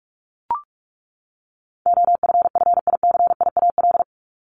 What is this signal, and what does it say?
Morse code, heard as a keyed tone, about 44 wpm, 700 Hz. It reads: OFFICIAL
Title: Top 751-1000 Words 45wpm